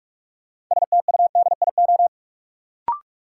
Morse code, heard as a keyed tone, about 46 wpm, 700 Hz. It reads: STUDIO
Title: Sentences from Top 1200 Words - New Word Drill Alt - 45wpm